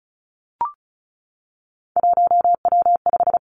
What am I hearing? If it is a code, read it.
Morse code, heard as a keyed tone, about 36 wpm, 700 Hz. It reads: 1W5
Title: Three Challenging Letter-Number Combinations 35wpm